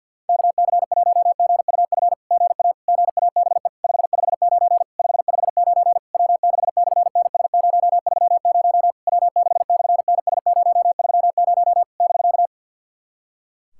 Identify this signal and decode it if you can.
Morse code, heard as a keyed tone, about 50 wpm, 700 Hz. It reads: KC1GUF GA GABE 559 559 W6/NS030 W6/NS030 <BK>
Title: SOTA QSO Practice 50wpm